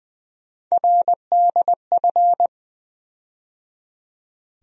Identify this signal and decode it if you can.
Morse code, heard as a keyed tone, about 20 wpm, 700 Hz. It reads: RDF
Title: USA Call Sign Postfixes - 3 Letters 20wpm